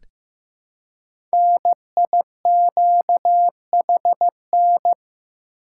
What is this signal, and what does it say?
Morse code, heard as a keyed tone, about 15 wpm, 700 Hz. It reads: NIQHN
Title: Code Groups - 5 Letter:Numbers - 15wpm